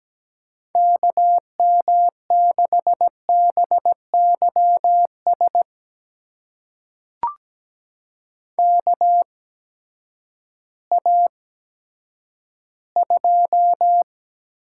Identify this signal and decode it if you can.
Morse code, heard as a keyed tone, about 17 wpm, spaced as for 10 wpm, 700 Hz. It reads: KM6BYS KA2
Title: USA Call Signs - 6 Characters - 8x Character Spacing - 17wpm